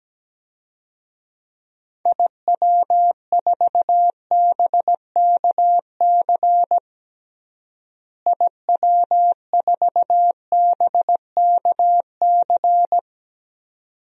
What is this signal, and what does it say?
Morse code, heard as a keyed tone, about 17 wpm, 700 Hz. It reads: IW4BKC IW4BKC
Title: DXCC-100 Call Signs - 6 Characters - Repeated 2x - 17wpm